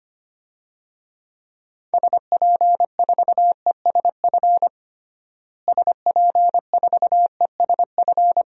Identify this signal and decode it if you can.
Morse code, heard as a keyed tone, about 25 wpm, 700 Hz. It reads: SP4ESF SP4ESF
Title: DXCC-100 Call Signs - 6 Characters - Repeated 2x - 25wpm